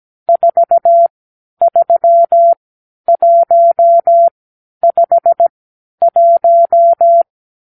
Morse code, heard as a keyed tone, about 17 wpm, spaced as for 10 wpm, 680 Hz. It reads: 43151
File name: k7qo / Lesson 50 (1-6)